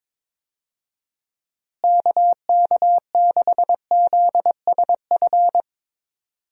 Morse code, heard as a keyed tone, about 21 wpm, 700 Hz. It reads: KK6ZSF
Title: USA Call Signs - 6 Characters - Repeated 2x - 22wpm